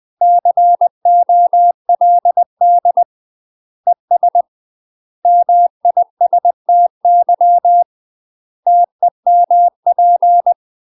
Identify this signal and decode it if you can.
Morse code, heard as a keyed tone, about 20 wpm, 700 Hz. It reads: COLD ES MISTY TEMP
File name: UK MASTER QSOs TRAINING / 20WPM / UK MASTER QSOs 0021